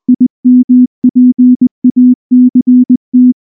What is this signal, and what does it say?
Morse code, heard as a keyed tone, about 20 wpm, 250 Hz. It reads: IMPACT
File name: morse-training-20WPM